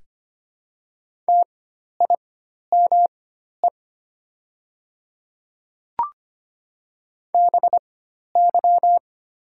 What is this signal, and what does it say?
Morse code, heard as a keyed tone, about 25 wpm, spaced as for 12 wpm, 700 Hz. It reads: TIME BY
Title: 4x Character Spacing - Top 100 Words - 25wpm